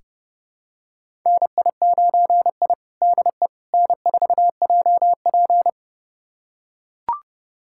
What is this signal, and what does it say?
Morse code, heard as a keyed tone, about 30 wpm, 700 Hz. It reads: NI9I DE N4JP
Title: USA Callsign DE Callsign - 4 Characters - 30wpm